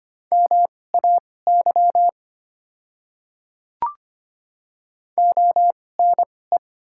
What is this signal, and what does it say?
Morse code, heard as a keyed tone, about 25 wpm, spaced as for 18 wpm, 690 Hz. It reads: MAY ONE
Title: 2x Character Spacing - Top 200 Words - 25wpm